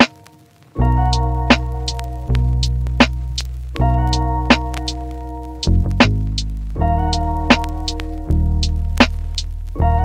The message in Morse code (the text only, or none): none